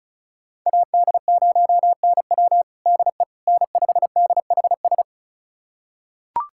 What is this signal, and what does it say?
Morse code, heard as a keyed tone, about 36 wpm, 700 Hz. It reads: AD0NW DE N5DHS
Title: USA Callsign DE Callsign - 5 Characters - 35wpm